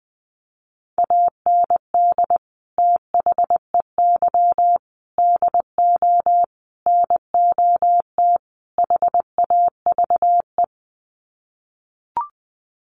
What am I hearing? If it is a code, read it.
Morse code, heard as a keyed tone, about 20 wpm, 700 Hz. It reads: AND THEY DO NOT HAVE
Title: Sets of 5 Words Encore2 20wpm